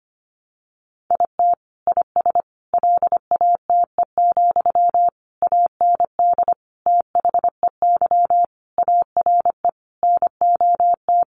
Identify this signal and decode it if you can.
Morse code, heard as a keyed tone, about 25 wpm, 700 Hz. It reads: IT IS LATE, AND THEY ARE NOT
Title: Sentences from Top 600 Words 25wpm